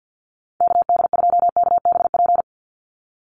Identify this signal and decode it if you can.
Morse code, heard as a keyed tone, about 50 wpm, 700 Hz. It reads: KD2KBP
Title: USA Call Signs 50wpm